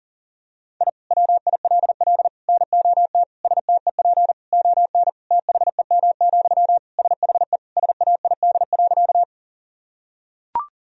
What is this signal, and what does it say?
Morse code, heard as a keyed tone, about 40 wpm, 700 Hz. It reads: I WILL NOT STEP ON THEM, SHE SAID.
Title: Sentences from Top 1000 Words - Review Part 2 - 40wpm